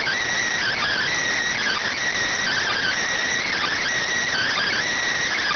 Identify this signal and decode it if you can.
Emergency Alert System (SAME) tones (header bursts), heard as harsh buzzing bursts. It is heard from 0.6 to 4.8 s.